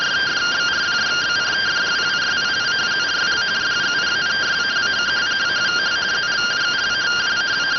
Two-tone teleprinter signal, about 170 Hz shift.